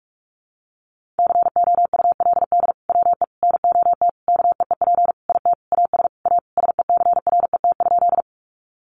Morse code, heard as a keyed tone, about 44 wpm, 700 Hz. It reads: COULD WE NOT KEEP IT AS A SECRET?
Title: Sentences from Top 1200 Words 45wpm